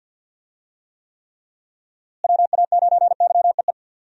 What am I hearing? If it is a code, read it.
Morse code, heard as a keyed tone, about 51 wpm, 700 Hz. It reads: WA9YEE
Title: USA Call Signs - 6 Characters - Repeated 3x - 50wpm